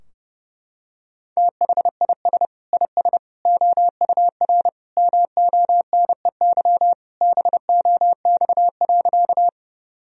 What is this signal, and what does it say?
Morse code, heard as a keyed tone, about 30 wpm, 700 Hz. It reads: THIS IS OUR MONEY BOX.
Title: Sentences from Top 1000 Words - Review Part 2 - 30wpm